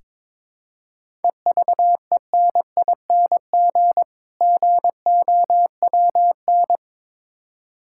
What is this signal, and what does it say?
Morse code, heard as a keyed tone, about 22 wpm, 690 Hz. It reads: EVENING GOWN
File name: Difficult Words with Hints - 22wpm